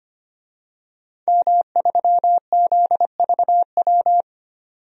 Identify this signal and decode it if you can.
Morse code, heard as a keyed tone, about 25 wpm, 700 Hz. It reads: M3ZVW
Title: DXCC-100 Call Signs - 5 Characters - 25wpm